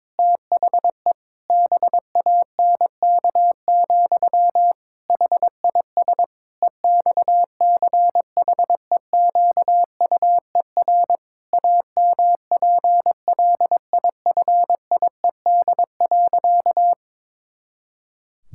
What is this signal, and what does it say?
Morse code, heard as a keyed tone, about 22 wpm, 700 Hz. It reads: THE BANK, HIS EXCHEQUER AMPLIFIED.
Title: Pangram Challenge v2 - 22wpm